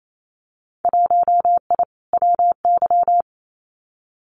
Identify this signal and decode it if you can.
Morse code, heard as a keyed tone, about 28 wpm, 700 Hz. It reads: 1I WY
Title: Winter Field Day Exchanges Encore 28wpm